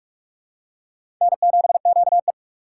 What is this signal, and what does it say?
Morse code, heard as a keyed tone, about 46 wpm, 700 Hz. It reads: N7QE
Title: USA Call Signs - 3x Character Spacing - 45wpm